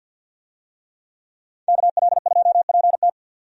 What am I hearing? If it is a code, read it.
Morse code, heard as a keyed tone, about 51 wpm, 700 Hz. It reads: KL2PT
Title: USA Call Signs - 5 Characters - 50wpm